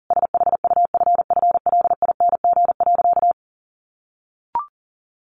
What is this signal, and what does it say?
Morse code, heard as a keyed tone, about 40 wpm, 700 Hz. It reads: SHUFFLING.
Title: A Sound Of Thunder 40wpm